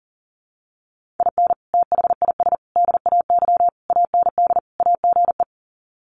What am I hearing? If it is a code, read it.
Morse code, heard as a keyed tone, about 40 wpm, 700 Hz. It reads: IN THIS DAY AND AGE
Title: Sets of 5 Words Encore 40wpm